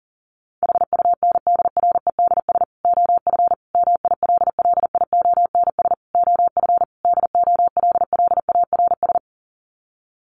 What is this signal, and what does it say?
Morse code, heard as a keyed tone, about 40 wpm, 700 Hz. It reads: HUNDREDS OF MILLIONS OF DOLLARS
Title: Sets of 5 Words 40wpm